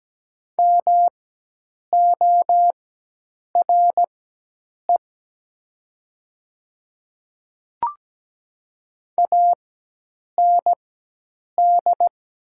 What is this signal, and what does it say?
Morse code, heard as a keyed tone, about 17 wpm, spaced as for 8 wpm, 690 Hz. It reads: MORE AND
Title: 4x Character Spacing - Top 200 Words - 17wpm